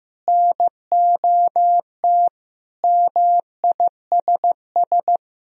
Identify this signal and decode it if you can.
Morse code, heard as a keyed tone, about 15 wpm, 700 Hz. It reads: NOT MISS
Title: Sentences from Top 1000 Words - Review Part 3 - 15wpm